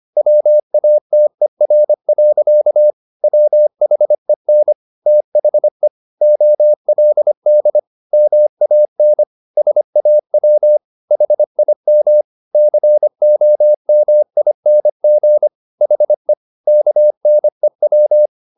Morse code, heard as a keyed tone, about 25 wpm, 600 Hz. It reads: WATER. WHEN THE OLD MAN SAW HIM COMING HE KNEW